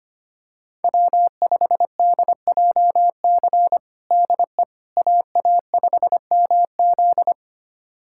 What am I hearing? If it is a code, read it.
Morse code, heard as a keyed tone, about 25 wpm, 700 Hz. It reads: W5DJC DE AA5MZ
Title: USA Callsign DE Callsign - 5 Characters - 25wpm